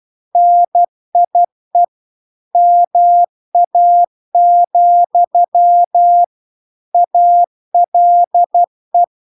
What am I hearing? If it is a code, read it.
Morse code, heard as a keyed tone, about 12 wpm, 700 Hz. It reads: NIE MA, ALE